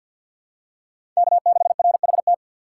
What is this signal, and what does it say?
Morse code, heard as a keyed tone, about 53 wpm, 700 Hz. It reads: K6RHT
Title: USA Call Signs - 5 Characters - 4x Character Spacing - 50wpm